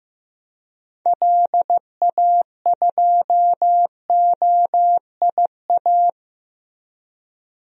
Morse code, heard as a keyed tone, about 15 wpm, 700 Hz. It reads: LA2OIA
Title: DXCC-100 Call Signs - 6 Characters - Repeated 2x - 15wpm